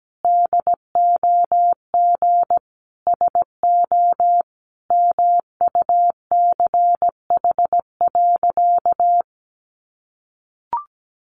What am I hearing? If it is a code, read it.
Morse code, heard as a keyed tone, about 17 wpm, 700 Hz. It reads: DOG SO MUCH.
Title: Additional Easy Phrases 17wpm